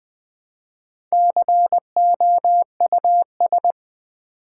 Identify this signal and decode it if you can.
Morse code, heard as a keyed tone, about 20 wpm, 700 Hz. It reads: COUS
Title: Build A Word - 2 Letters At A Time - 6-Letter Words - 20wpm